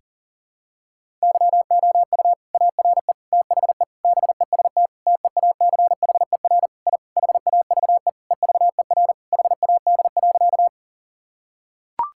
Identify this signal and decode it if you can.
Morse code, heard as a keyed tone, about 40 wpm, 700 Hz. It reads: YOU ARE THE BEST TEACHER I HAVE EVER HAD.
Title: Sentences from Top 1000 Words - Review Part 6 - 40wpm